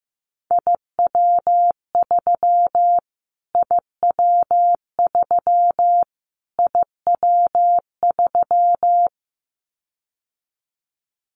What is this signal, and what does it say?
Morse code, heard as a keyed tone, about 15 wpm, 700 Hz. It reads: IW3 IW3 IW3
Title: Callsign Prefixes for Top 25 DXCC - Repeated 3x - 15wpm